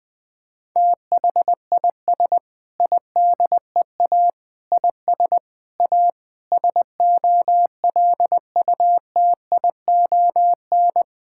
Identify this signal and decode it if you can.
Morse code, heard as a keyed tone, about 20 wpm, 700 Hz. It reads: THIS IDEA IS A SOLUTION
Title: Sentences with 8-Word Phrases - 20wpm